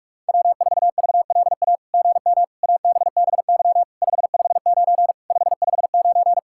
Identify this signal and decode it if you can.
Morse code, heard as a keyed tone, about 44 wpm, 700 Hz. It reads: W4VLA GM ABBY 559 559
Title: SOTA QSO Practice 45wpm